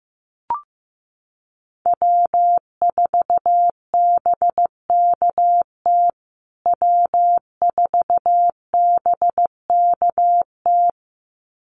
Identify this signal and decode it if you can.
Morse code, heard as a keyed tone, about 15 wpm, 700 Hz. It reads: W4BKT W4BKT
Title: USA Call Signs Repeated 2x 15wpm